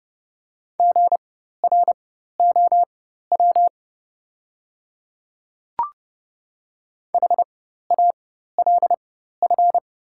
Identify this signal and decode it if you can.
Morse code, heard as a keyed tone, about 30 wpm, spaced as for 14 wpm, 690 Hz. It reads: GROW HALF
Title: 4x Character Spacing - Top 500 Words - 30wpm